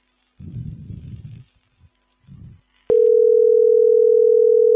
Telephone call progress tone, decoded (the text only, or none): ringback tone